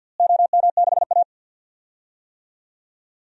Morse code, heard as a keyed tone, about 53 wpm, 700 Hz. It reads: OM6A
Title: DXCC-100 Call Signs - 4 Characters - Repeated 3x - 50wpm